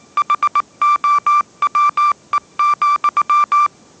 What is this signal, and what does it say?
Morse code, heard as a keyed tone, about 19 wpm, 1180 Hz. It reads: HOWE,